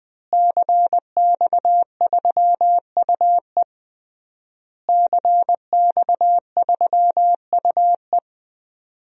Morse code, heard as a keyed tone, about 20 wpm, 700 Hz. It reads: CX3UE CX3UE
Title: DXCC-100 Call Signs - 5 Characters - Repeated 3x - 20wpm